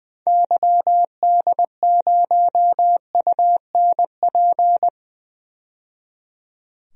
Morse code, heard as a keyed tone, about 20 wpm, 700 Hz. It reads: YD0UNP
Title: DXCC-100 Call Signs - 6 Characters - 20wpm